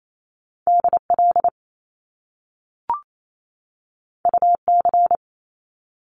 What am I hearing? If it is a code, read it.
Morse code, heard as a keyed tone, about 28 wpm, 700 Hz. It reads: DL UC
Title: Top Two Letter Combinations in Words 28wpm